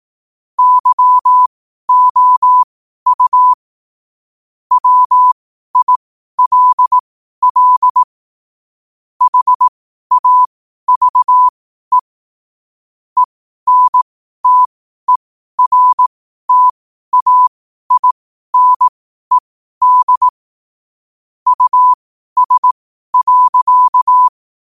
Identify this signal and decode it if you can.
Morse code, heard as a keyed tone, about 18 wpm, spaced as for 12 wpm, 1000 Hz. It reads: YOU WILL HAVE ENTERTAINED US.
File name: 12wpm